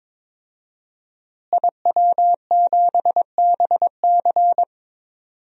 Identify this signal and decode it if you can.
Morse code, heard as a keyed tone, about 21 wpm, 700 Hz. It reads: IW7BC
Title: DXCC-100 Call Signs - 5 Characters - Repeated 3x - 22wpm